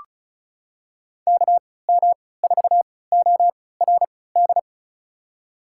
Morse code, heard as a keyed tone, about 36 wpm, spaced as for 20 wpm, 700 Hz. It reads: KM4ORD
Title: USA Call Signs - 6 Characters - 3x Character Spacing - 35wpm